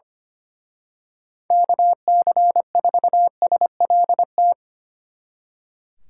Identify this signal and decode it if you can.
Morse code, heard as a keyed tone, about 28 wpm, 700 Hz. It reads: KC4SLT
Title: Speed Racing - USA 2x3 Call Signs 25wpm